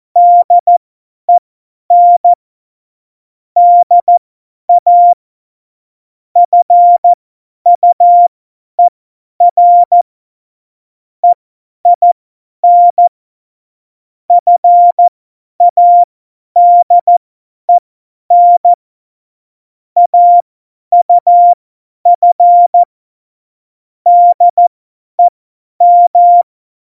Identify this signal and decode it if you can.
Morse code, heard as a keyed tone, about 14 wpm, spaced as for 10 wpm, 700 Hz. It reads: DEN DA FUER EIN FADEN AUF DEM
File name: Morsekurs / Witze / Witz 02. Juli 2019, 50 F70